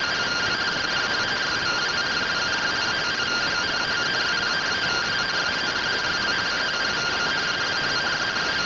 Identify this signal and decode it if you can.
Two-tone teleprinter signal, about 170 Hz shift.